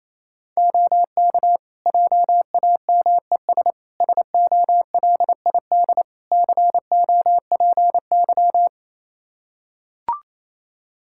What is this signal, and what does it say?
Morse code, heard as a keyed tone, about 28 wpm, 700 Hz. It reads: OK JAMES SOLID COPY
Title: Practice QSO - Exchange 3 - More Info - 28wpm